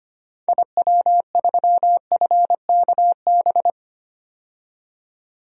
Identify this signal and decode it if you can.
Morse code, heard as a keyed tone, about 25 wpm, 700 Hz. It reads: IW3FKB
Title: DXCC-100 Call Signs - 6 Characters - 25wpm